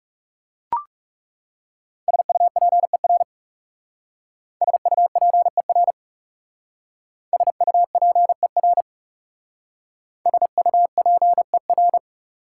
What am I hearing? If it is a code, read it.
Morse code, heard as a keyed tone, about 36 wpm, 690 Hz. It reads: SUPER SUPER SUPER SUPER
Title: Speed-Racing Prefixes and Suffixes 30wpm